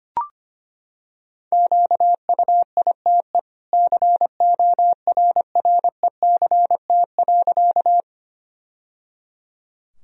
Morse code, heard as a keyed tone, about 25 wpm, 700 Hz. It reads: QUITE CORRECT.
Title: Sentences from Top 1000 Words - Review Part 2 - 25wpm